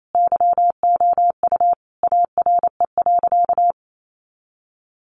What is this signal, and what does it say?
Morse code, heard as a keyed tone, about 28 wpm, 700 Hz. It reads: YOU ARE.
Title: Sentences from Top 1200 Words 28wpm